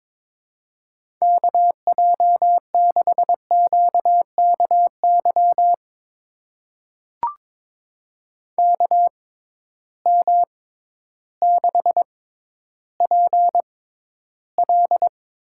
Morse code, heard as a keyed tone, about 22 wpm, spaced as for 12 wpm, 700 Hz. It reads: KJ6QKY KM6PL
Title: USA Call Signs - 6 Characters - 6x Character Spacing - 22wpm